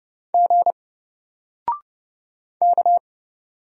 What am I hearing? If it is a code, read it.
Morse code, heard as a keyed tone, about 30 wpm, 700 Hz. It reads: G K
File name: Single-Letter-Number-30wpm